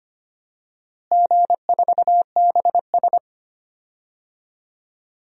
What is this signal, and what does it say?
Morse code, heard as a keyed tone, about 25 wpm, 700 Hz. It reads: G4BS
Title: DXCC-100 Call Signs - 4 Characters - Repeated 2x - 25wpm